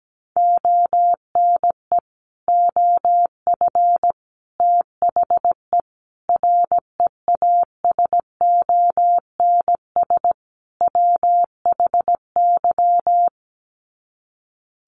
Morse code, heard as a keyed tone, about 17 wpm, 700 Hz. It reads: ONE OF THE REASONS WHY
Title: Sets of 5 Words 17wpm